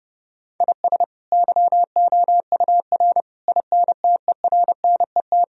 Morse code, heard as a keyed tone, about 30 wpm, 700 Hz. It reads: IS YOUR INTERNET
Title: Sentences from Top 1000 Words - Review Part 4 - 30wpm